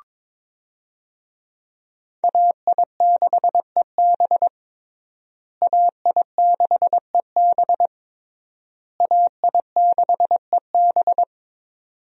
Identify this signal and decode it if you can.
Morse code, heard as a keyed tone, about 22 wpm, 700 Hz. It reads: AI6EB AI6EB AI6EB
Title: USA Call Signs - 5 Characters - Repeated 3x - 22wpm